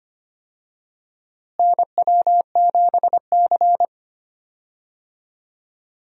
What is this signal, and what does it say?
Morse code, heard as a keyed tone, about 25 wpm, 700 Hz. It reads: NW7C
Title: USA Call Signs - 4 Characters - Repeated 3x - 25wpm